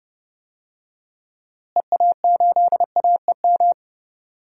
Morse code, heard as a keyed tone, about 30 wpm, 700 Hz. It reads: EA8AEM
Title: DXCC-100 Call Signs - 6 Characters - Repeated 3x - 30wpm